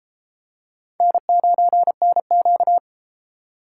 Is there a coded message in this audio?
Yes: Morse code from 1.0 to 2.8 s, heard as a keyed tone, about 32 wpm, 700 Hz. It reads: N9NQ